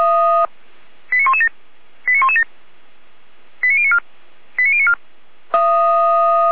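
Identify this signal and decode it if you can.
Five-tone selective call (ZVEI), heard as quick runs of five tones, tones of about 70 ms. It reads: 89117 89117 89094 89094
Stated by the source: ZVEI five-tone selective call 89115 received via SDR.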